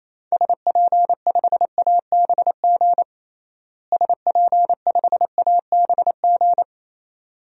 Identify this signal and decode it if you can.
Morse code, heard as a keyed tone, about 28 wpm, 700 Hz. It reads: SP5ABG SP5ABG
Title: DXCC-100 Call Signs - 6 Characters - Repeated 2x - 28wpm